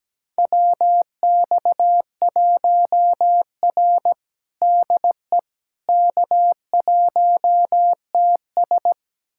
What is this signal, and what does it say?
Morse code, heard as a keyed tone, about 17 wpm, 700 Hz. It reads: WX1R DE K1TS
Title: USA Callsign DE Callsign - 4 Characters - 17wpm